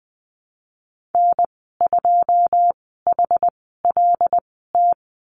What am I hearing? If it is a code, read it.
Morse code, heard as a keyed tone, about 20 wpm, spaced as for 15 wpm, 700 Hz. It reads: N2HLT
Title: 147 Lesson 24g - Letter y - Call Signs 20wpm